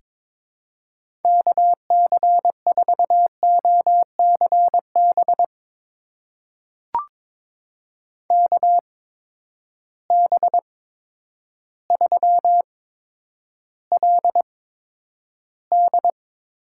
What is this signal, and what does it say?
Morse code, heard as a keyed tone, about 22 wpm, spaced as for 10 wpm, 700 Hz. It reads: KC4OCB KB3LD
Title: USA Call Signs - 6 Characters - 8x Character Spacing - 22wpm